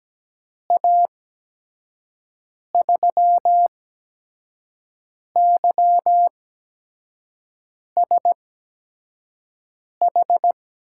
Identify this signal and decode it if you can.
Morse code, heard as a keyed tone, about 17 wpm, spaced as for 5 wpm, 700 Hz. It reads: A3YSH
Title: USA Call Signs - 6 Characters - 8x Character Spacing - 17wpm